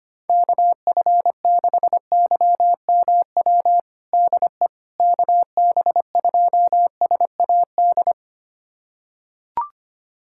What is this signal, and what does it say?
Morse code, heard as a keyed tone, about 25 wpm, 700 Hz. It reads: KF6YMW DE KB2SAD
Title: USA Callsign DE Callsign 25wpm